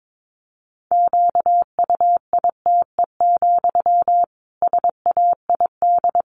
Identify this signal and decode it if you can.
Morse code, heard as a keyed tone, about 22 wpm, 700 Hz. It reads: QUITE, SAID
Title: Sentences from Top 1500 Words 22wpm